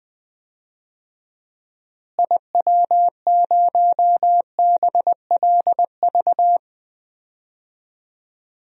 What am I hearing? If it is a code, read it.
Morse code, heard as a keyed tone, about 20 wpm, 700 Hz. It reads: IW0BLV
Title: DXCC-100 Call Signs - 6 Characters - Repeated 2x - 20wpm